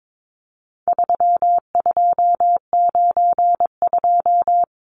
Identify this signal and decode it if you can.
Morse code, heard as a keyed tone, about 22 wpm, 700 Hz. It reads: 3292
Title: Numbers - 4-Digit 22wpm